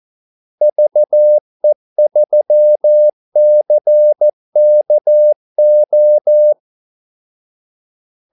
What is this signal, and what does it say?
Morse code, heard as a keyed tone, about 14 wpm, 600 Hz. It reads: VE3CKO
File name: Callsigns-14wpm